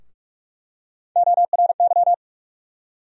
Morse code, heard as a keyed tone, about 46 wpm, 700 Hz. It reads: ORY
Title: USA Call Sign Postfixes - 3 Letters 45wpm